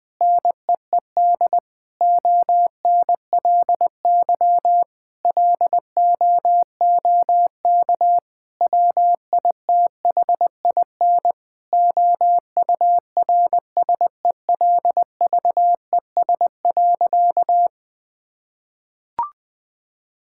Morse code, heard as a keyed tone, about 21 wpm, 700 Hz. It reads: NEED ONLY LOOK WITHIN OURSELVES.
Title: Sentences from Top 1600 Words 20wpm